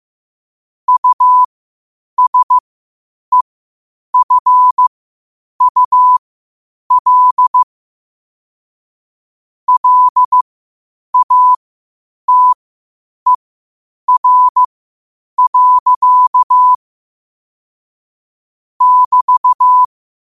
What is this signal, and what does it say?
Morse code, heard as a keyed tone, about 15 wpm, spaced as for 8 wpm, 1000 Hz. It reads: USEFUL LATER. <BT>
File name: qotdfiles / 8wpm